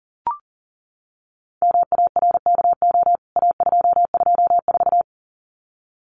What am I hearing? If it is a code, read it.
Morse code, heard as a keyed tone, about 40 wpm, 700 Hz. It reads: MARKO A224
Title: CWT Exchange 40wpm